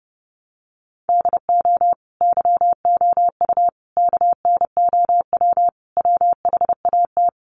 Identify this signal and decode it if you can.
Morse code, heard as a keyed tone, about 30 wpm, 700 Hz. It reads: DO YOU KNOW WHAT
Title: Sentences from Top 1200 Words 30wpm